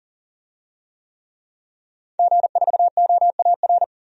Morse code, heard as a keyed tone, about 40 wpm, 700 Hz. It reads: G4OAR
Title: DXCC-100 Call Signs - 5 Characters - 40wpm